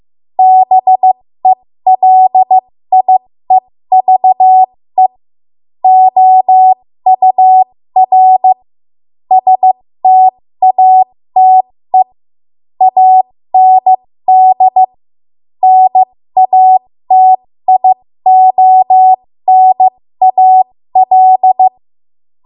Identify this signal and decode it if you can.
Morse code, heard as a keyed tone, about 15 wpm, spaced as for 13 wpm, 750 Hz. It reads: BELIEVE OUR STATE AND NATIONAL
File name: Archive / 13 WPM / 13WPM